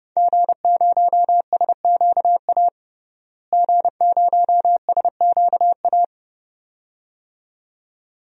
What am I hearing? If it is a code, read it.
Morse code, heard as a keyed tone, about 30 wpm, 700 Hz. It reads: G0SQA G0SQA
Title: DXCC-100 Call Signs - 5 Characters - Repeated 2x - 30wpm